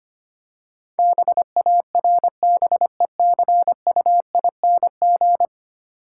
Morse code, heard as a keyed tone, about 25 wpm, 690 Hz. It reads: BARBECUING
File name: Easy Words - Speed-Racing with Hints - 25wpm